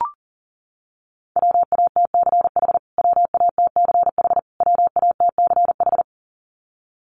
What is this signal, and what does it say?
Morse code, heard as a keyed tone, about 40 wpm, 700 Hz. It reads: WATCH WATCH WATCH
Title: Top 500 Words Repeated 40wpm